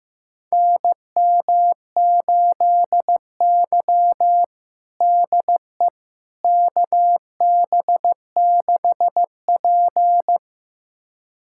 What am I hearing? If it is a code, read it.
Morse code, heard as a keyed tone, about 16 wpm, 700 Hz. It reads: NM8Y DE KB6P
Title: USA Callsign DE Callsign - 4 Characters - 15wpm